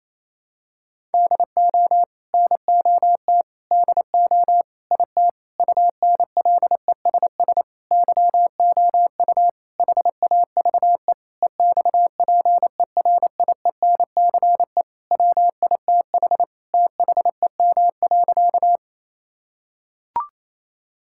Morse code, heard as a keyed tone, about 28 wpm, 700 Hz. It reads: DO NOT DO IT UNLESS YOU HAVE EXPERIENCE WITH THEM.
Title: Sentences from Top 1000 Words - Review Part 6 - 28wpm